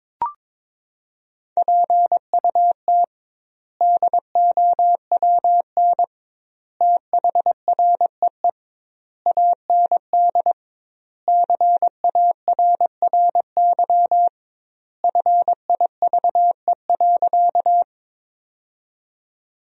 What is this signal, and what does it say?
Morse code, heard as a keyed tone, about 22 wpm, 700 Hz. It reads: PUT DOWN THREE AND CARRY FIVE.
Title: 2x Word-Spacing - Sentences from Top 400 Words - 22wpm